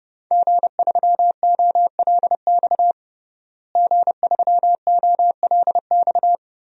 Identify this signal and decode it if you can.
Morse code, heard as a keyed tone, about 30 wpm, 700 Hz. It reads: G3OLX G3OLX
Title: DXCC-100 Call Signs - 5 Characters - Repeated 3x - 30wpm